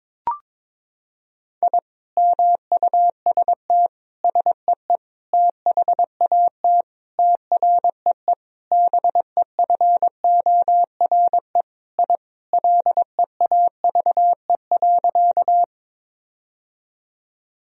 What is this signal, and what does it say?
Morse code, heard as a keyed tone, about 22 wpm, 700 Hz. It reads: I MUST SEE THAT TREE BEFORE I LEAVE.
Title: Sentences from Top 1000 Words - Review Part 1 - 22wpm